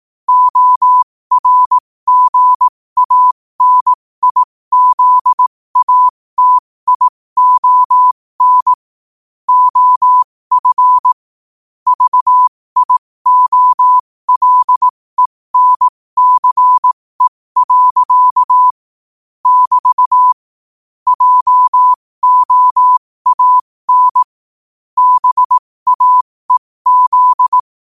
Morse code, heard as a keyed tone, about 18 wpm, spaced as for 15 wpm, 1000 Hz. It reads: ORGANIZATION OF VIOLENCE. <BT> JOAN BAEZ
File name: qotdfiles / 15wpm